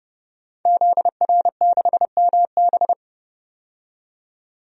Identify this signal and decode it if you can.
Morse code, heard as a keyed tone, about 30 wpm, 700 Hz. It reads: ZR6MB
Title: DXCC-100 Call Signs - 5 Characters - 30wpm